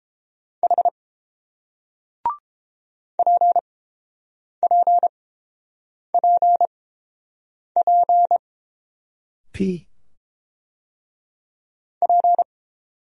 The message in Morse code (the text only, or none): H PPPP P